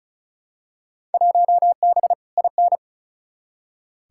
Morse code, heard as a keyed tone, about 34 wpm, 700 Hz. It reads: 1B IN
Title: Field Day Exchanges - 35wpm